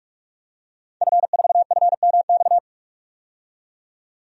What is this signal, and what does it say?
Morse code, heard as a keyed tone, about 46 wpm, 700 Hz. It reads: F4FMX
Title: DXCC-100 Call Signs - 5 Characters - 45wpm